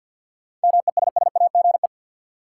Morse code, heard as a keyed tone, about 52 wpm, 700 Hz. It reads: MESSAGE
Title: Sentences from Top 1100 Words - New Word Drill - 50wpm